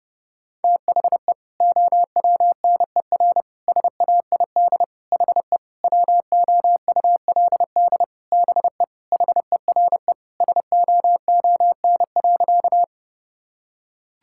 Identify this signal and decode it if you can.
Morse code, heard as a keyed tone, about 30 wpm, 700 Hz. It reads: THE OWNER SAID HE WOULD BE HERE SOON.
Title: Sentences from Top 1600 Words 30wpm